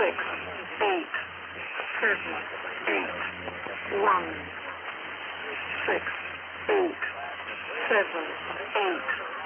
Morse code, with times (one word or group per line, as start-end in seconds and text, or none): none